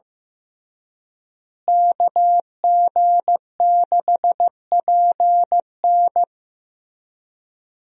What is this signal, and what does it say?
Morse code, heard as a keyed tone, about 15 wpm, 700 Hz. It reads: KG6PN
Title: USA Call Signs - 5 Characters - Repeated 3x - 15wpm